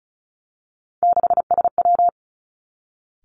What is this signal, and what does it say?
Morse code, heard as a keyed tone, about 36 wpm, 700 Hz. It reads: 6SW
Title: Three Challenging Letter-Number Combinations 35wpm